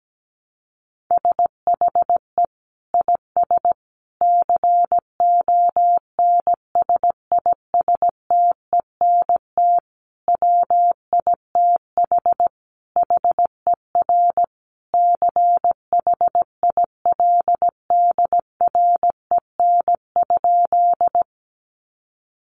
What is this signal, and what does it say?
Morse code, heard as a keyed tone, about 17 wpm, 700 Hz. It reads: SHE IS CONSISTENT WITH HER CHILDREN?
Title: Sentences from Top 1500 Words 17wpm